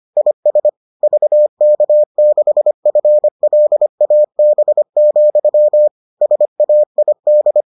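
Morse code, heard as a keyed tone, about 25 wpm, 600 Hz. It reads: IS VK6FLAB, SAID
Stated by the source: Morse edition of the Foundations of Amateur Radio episodes.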